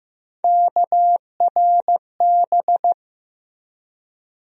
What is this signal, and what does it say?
Morse code, heard as a keyed tone, about 15 wpm, 700 Hz. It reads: KRB
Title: USA Call Sign Postfixes - 3 Letters 15wpm